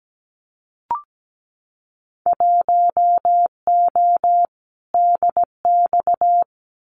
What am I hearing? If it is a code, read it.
Morse code, heard as a keyed tone, about 17 wpm, 700 Hz. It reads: 1O DX
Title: Winter Field Day Exchanges 17wpm